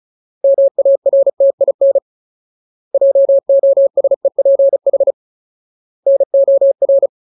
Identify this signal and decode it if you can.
Morse code, heard as a keyed tone, about 34 wpm, 550 Hz. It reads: MARTIN JOSEPH NOR
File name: Ultimate Word List / 35WPM Ultimate